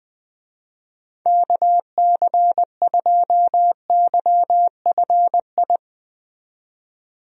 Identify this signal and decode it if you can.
Morse code, heard as a keyed tone, about 20 wpm, 700 Hz. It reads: KC2YFI
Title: USA Call Signs - 6 Characters - 20wpm